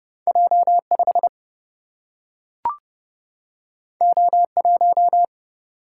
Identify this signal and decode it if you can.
Morse code, heard as a keyed tone, about 30 wpm, 700 Hz. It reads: J 5 O 1
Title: Single-Letter Single-Number 30wpm